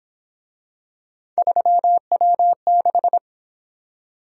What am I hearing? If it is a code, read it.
Morse code, heard as a keyed tone, about 26 wpm, 700 Hz. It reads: 3W6
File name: Speed-Racing - Three Letter-Number Combinations - 22wpm